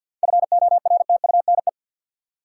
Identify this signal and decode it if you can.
Morse code, heard as a keyed tone, about 50 wpm, 700 Hz. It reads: FORTUNE
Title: Seven Letter Words - 50wpm